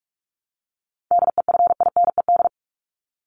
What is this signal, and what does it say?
Morse code, heard as a keyed tone, about 44 wpm, 700 Hz. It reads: DEFINED
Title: ED Suffix - Top 100 Words with Base Words 45wpm